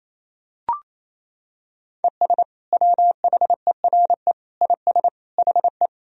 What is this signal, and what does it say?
Morse code, heard as a keyed tone, about 28 wpm, 700 Hz. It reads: ES WHERE IS HE
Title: Sentences from Top 200 Words and Abbreviations 28wpm